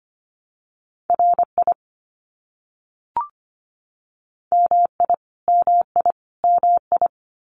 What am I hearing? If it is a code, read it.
Morse code, heard as a keyed tone, about 25 wpm, 700 Hz. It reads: RI MI MI MI
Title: US States Repeated 3x 25wpm